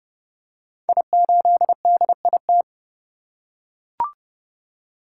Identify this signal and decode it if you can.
Morse code, heard as a keyed tone, about 28 wpm, 700 Hz. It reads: I8DIT
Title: Code Groups - 5 Letter:Numbers - 30wpm